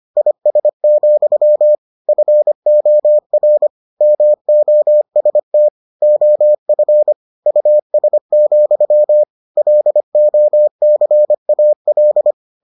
Morse code, heard as a keyed tone, about 25 wpm, 600 Hz. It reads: IS, FOR MOST OF US, LOCAL